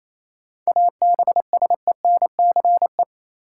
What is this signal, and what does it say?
Morse code, heard as a keyed tone, about 28 wpm, 700 Hz. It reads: ABSENCE
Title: Sets of 4 Words - New Word Drill - 28wpm